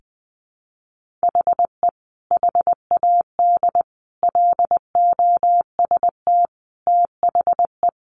Morse code, heard as a keyed tone, about 19 wpm, 700 Hz. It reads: HE HAD LOST THE
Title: Sentences from Top 1500 Words 20wpm